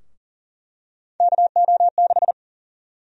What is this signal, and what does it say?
Morse code, heard as a keyed tone, about 40 wpm, 700 Hz. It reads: KO6
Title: USA Call Sign Prefixes - 40wpm